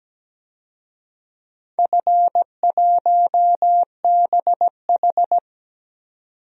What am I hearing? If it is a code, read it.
Morse code, heard as a keyed tone, about 17 wpm, 700 Hz. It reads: F1BH
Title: DXCC-100 Call Signs - 4 Characters - 17wpm